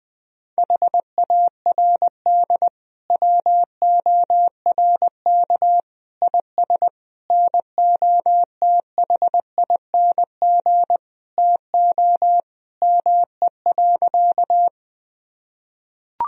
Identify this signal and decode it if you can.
Morse code, heard as a keyed tone, about 20 wpm, 700 Hz. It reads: HARD WORK IS NOTHING TO ME.
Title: Sentences from Top 1000 Words - Review Part 1 - 20wpm